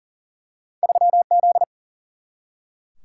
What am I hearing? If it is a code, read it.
Morse code, heard as a keyed tone, about 40 wpm, 700 Hz. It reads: 3Z
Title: Single-Letter Single-Number 40wpm